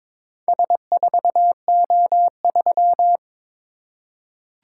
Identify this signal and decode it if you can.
Morse code, heard as a keyed tone, about 23 wpm, 700 Hz. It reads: S4O3
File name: practice / 22wpm / Four Letter-Number Combinations - 22wpm